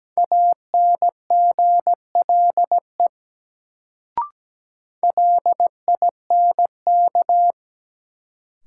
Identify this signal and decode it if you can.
Morse code, heard as a keyed tone, about 17 wpm, 700 Hz. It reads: ANGLE LINK
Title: Sentences from Top 1500 Words - New Word Drill Alt - 17wpm